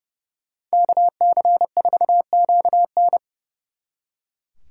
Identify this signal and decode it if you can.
Morse code, heard as a keyed tone, about 29 wpm, 700 Hz. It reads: KC4QN
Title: Speed Racing - USA Call Signs 30wpm